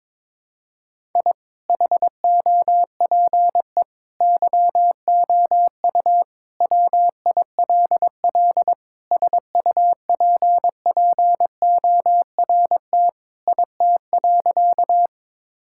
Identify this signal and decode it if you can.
Morse code, heard as a keyed tone, about 22 wpm, 700 Hz. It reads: I HOPE YOU WILL SUPPORT IT.
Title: Sentences from Top 1000 Words - Review Part 5 - 22wpm